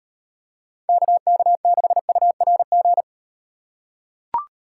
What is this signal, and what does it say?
Morse code, heard as a keyed tone, about 38 wpm, 700 Hz. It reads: KK6URG